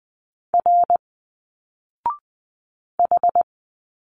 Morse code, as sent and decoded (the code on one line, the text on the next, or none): .-. / ....
R H